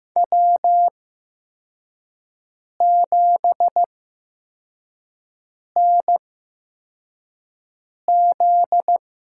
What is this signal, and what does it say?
Morse code, heard as a keyed tone, about 15 wpm, spaced as for 4 wpm, 700 Hz. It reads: W7NZ
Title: USA Call Signs - 8x Character Spacing - 15wpm